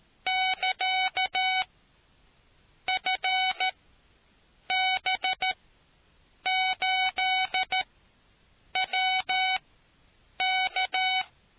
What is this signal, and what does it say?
Morse code, heard as a keyed tone, about 13 wpm, spaced as for 7 wpm, 750 Hz. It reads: <KA> FB8WK